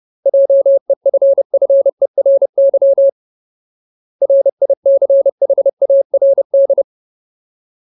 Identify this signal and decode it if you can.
Morse code, heard as a keyed tone, about 30 wpm, 550 Hz. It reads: JEFFERY RICHARD
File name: Ultimate Word List / 30WPM Ultimate